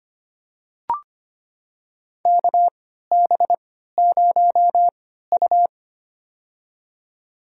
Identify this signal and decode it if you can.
Morse code, heard as a keyed tone, about 25 wpm, spaced as for 14 wpm, 700 Hz. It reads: KB0U
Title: USA Call Signs - 4 Characters - 3x Character Spacing - 25wpm